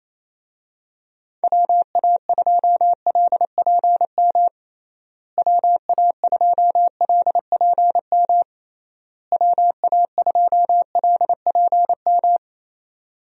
Morse code, heard as a keyed tone, about 28 wpm, 700 Hz. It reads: WA2LPM WA2LPM WA2LPM
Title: USA Call Signs - 6 Characters - Repeated 3x - 28wpm